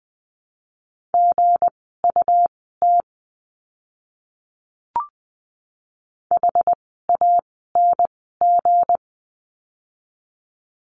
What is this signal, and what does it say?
Morse code, heard as a keyed tone, about 20 wpm, spaced as for 15 wpm, 700 Hz. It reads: GUT HANG
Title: 160 Lesson 26f - Letter g - Words 20wpm